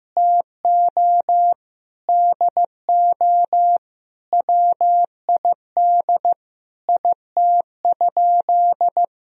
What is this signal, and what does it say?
Morse code, heard as a keyed tone, about 15 wpm, 700 Hz. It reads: TO DO WID IT?
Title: Sentences from Top 100 Words and Abbreviations 15wpm